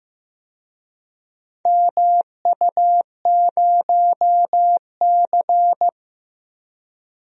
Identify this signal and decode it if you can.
Morse code, heard as a keyed tone, about 15 wpm, 700 Hz. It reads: MU0C
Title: DXCC-100 Call Signs - 4 Characters - 15wpm